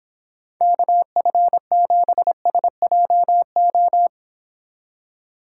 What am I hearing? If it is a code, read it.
Morse code, heard as a keyed tone, about 30 wpm, 700 Hz. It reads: KF7SJO